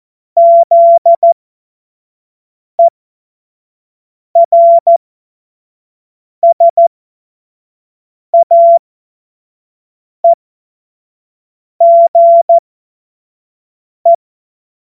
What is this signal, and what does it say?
Morse code, heard as a keyed tone, about 14 wpm, spaced as for 5 wpm, 680 Hz. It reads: ZERSAEGE
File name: Morsekurs / Witze / Witz 28.November 2023 25 F70